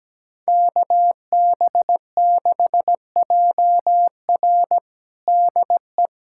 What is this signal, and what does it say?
Morse code, heard as a keyed tone, about 17 wpm, 700 Hz. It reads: KB6JR DE
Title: USA Callsign DE Callsign 17wpm